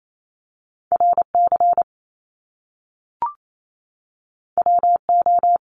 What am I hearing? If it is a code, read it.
Morse code, heard as a keyed tone, about 28 wpm, 700 Hz. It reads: RC WO
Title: Top Two Letter Combinations in Words 28wpm